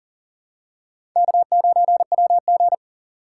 Morse code, heard as a keyed tone, about 40 wpm, 700 Hz. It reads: K9WG
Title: USA Call Signs - 4 Characters - 2x Character Spacing - 40wpm